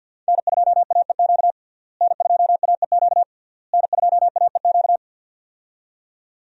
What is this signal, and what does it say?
Morse code, heard as a keyed tone, about 50 wpm, 700 Hz. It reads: N2AEQ N2AEQ N2AEQ
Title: USA Call Signs - 5 Characters - Repeated 3x - 50wpm